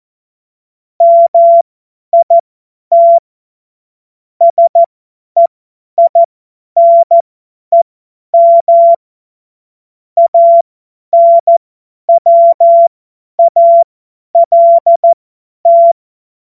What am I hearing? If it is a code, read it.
Morse code, heard as a keyed tone, about 14 wpm, spaced as for 10 wpm, 680 Hz. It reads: MIT SEINEM ANWALT
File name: Morsekurs / Witze / Witz 23. August 2022 50 F70